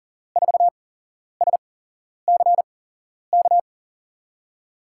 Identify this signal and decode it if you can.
Morse code, heard as a keyed tone, about 40 wpm, spaced as for 11 wpm, 700 Hz. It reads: 4SCK
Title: USA Call Signs - 6 Characters - 8x Character Spacing - 40wpm